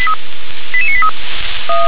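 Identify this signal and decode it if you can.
Five-tone selective call (ZVEI), heard as quick runs of five tones, tones of about 70 ms. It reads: 88983
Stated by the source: ZVEI five-tone call 88840 received on SDR.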